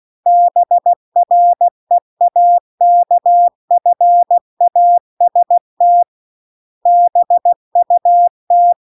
Morse code, heard as a keyed tone, about 16 wpm, 700 Hz. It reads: BREAKFAST BUT
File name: CW-The-Big-Twoheaded-River-2-16WPM